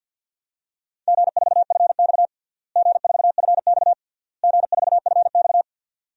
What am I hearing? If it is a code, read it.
Morse code, heard as a keyed tone, about 52 wpm, 700 Hz. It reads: G4FX G4FX G4FX
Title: DXCC-100 Call Signs - 4 Characters - Repeated 3x - 50wpm